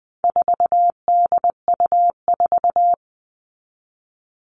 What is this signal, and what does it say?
Morse code, heard as a keyed tone, about 20 wpm, 700 Hz. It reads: 4DU4
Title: All Cut Numbers - 4-Digit 20wpm